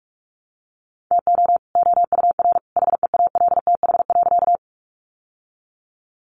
Morse code, heard as a keyed tone, about 44 wpm, 700 Hz. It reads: TO OUR HEALTH.
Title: Sentences from Top 600 Words 45wpm